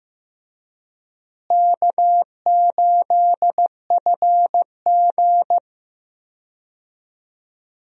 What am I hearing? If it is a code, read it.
Morse code, heard as a keyed tone, about 15 wpm, 700 Hz. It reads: K8FG
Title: USA Call Signs - 4 Characters - Repeated 3x - 15wpm